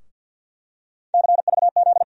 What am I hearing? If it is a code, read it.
Morse code, heard as a keyed tone, about 50 wpm, 700 Hz. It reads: CVZ
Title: USA Call Sign Postfixes - 3 Letters 50wpm